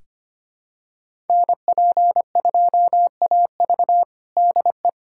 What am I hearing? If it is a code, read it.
Morse code, heard as a keyed tone, about 25 wpm, 700 Hz. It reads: NP2AV DE
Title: USA Callsign DE Callsign 25wpm